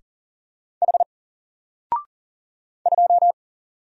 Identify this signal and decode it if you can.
Morse code, heard as a keyed tone, about 38 wpm, 700 Hz. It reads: H 2
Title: Single Character (Letter:Number:Punctuation:Prosign) Alt 40wpm